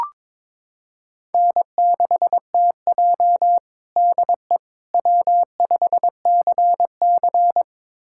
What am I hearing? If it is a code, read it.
Morse code, heard as a keyed tone, about 22 wpm, 700 Hz. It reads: N6TJ DE W5CC
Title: USA Callsign DE Callsign - 4 Characters - 22wpm